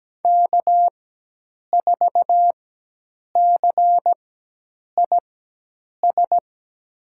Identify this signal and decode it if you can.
Morse code, heard as a keyed tone, about 17 wpm, spaced as for 8 wpm, 700 Hz. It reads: K4CIS
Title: USA Call Signs - 5 Characters - 4x Character Spacing - 17wpm